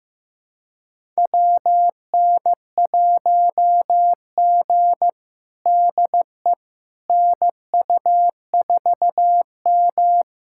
Morse code, heard as a keyed tone, about 15 wpm, 700 Hz. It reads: WN1G DE NU4M
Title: USA Callsign DE Callsign - 4 Characters - 15wpm